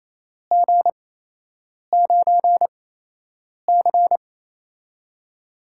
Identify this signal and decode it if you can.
Morse code, heard as a keyed tone, about 28 wpm, spaced as for 8 wpm, 700 Hz. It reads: G9C
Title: USA Call Signs - 4 Characters - 8x Character Spacing - 28wpm